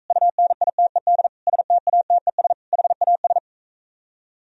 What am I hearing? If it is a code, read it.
Morse code, heard as a keyed tone, about 45 wpm, 700 Hz. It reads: UNITED STATES HAS